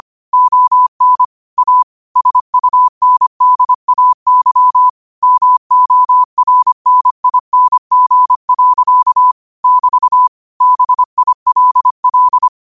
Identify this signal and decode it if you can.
Morse code, heard as a keyed tone, about 25 wpm, 1000 Hz. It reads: ON A SUNDAY MORNING. <BT> BILL